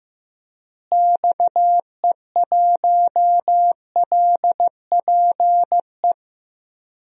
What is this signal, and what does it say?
Morse code, heard as a keyed tone, about 15 wpm, 700 Hz. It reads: XE1LPE
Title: DXCC-100 Call Signs - 6 Characters - Repeated 3x - 15wpm